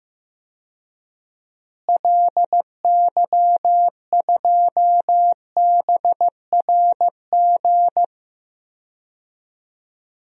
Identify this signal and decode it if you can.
Morse code, heard as a keyed tone, about 15 wpm, 700 Hz. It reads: LY2BRG
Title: DXCC-100 Call Signs - 6 Characters - Repeated 3x - 15wpm